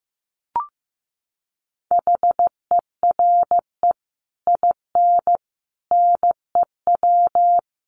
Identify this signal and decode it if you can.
Morse code, heard as a keyed tone, about 15 wpm, 700 Hz. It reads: HERE IN NEW
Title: Sets of 5 Words Encore2 15wpm